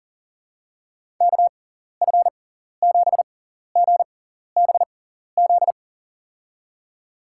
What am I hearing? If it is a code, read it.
Morse code, heard as a keyed tone, about 40 wpm, spaced as for 14 wpm, 700 Hz. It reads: KF7GBZ
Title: USA Call Signs - 6 Characters - 6x Character Spacing - 40wpm